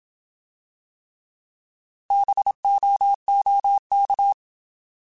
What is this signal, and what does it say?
Morse code, heard as a keyed tone, about 27 wpm, 800 Hz. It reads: BOOK
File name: Alphabetical / Alphabetical mp3s / Bs @25